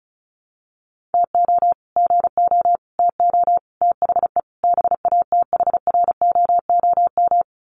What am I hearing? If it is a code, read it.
Morse code, heard as a keyed tone, about 34 wpm, 700 Hz. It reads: TO GO TO THE BATHROOM
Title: Sets of 5 Words Encore 35wpm